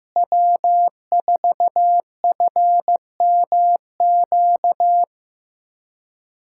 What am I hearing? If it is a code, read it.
Morse code, heard as a keyed tone, about 16 wpm, 700 Hz. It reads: W4FMQ
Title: USA Callsign DE Callsign - 5 Characters - 15wpm